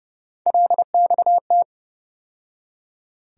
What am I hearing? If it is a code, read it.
Morse code, heard as a keyed tone, about 29 wpm, 700 Hz. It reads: LXT
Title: USA Call Sign Postfixes - 3 Letters 30wpm